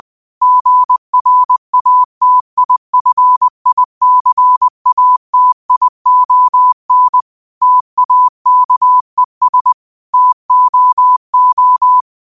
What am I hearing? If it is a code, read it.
Morse code, heard as a keyed tone, about 20 wpm, 1000 Hz. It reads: GRATIFICATION TAKES TOO
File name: qotdfiles / 20wpm